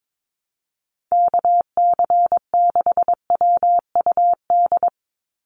Morse code, heard as a keyed tone, about 22 wpm, 700 Hz. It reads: KC6WUD
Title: USA Call Signs 22wpm